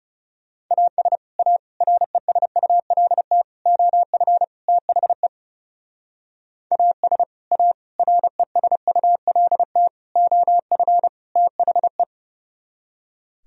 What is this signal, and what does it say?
Morse code, heard as a keyed tone, about 37 wpm, 700 Hz. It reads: AS A RESULT OF THE AS A RESULT OF THE
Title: Speed-Racing - Sets of 5 Words 30wpm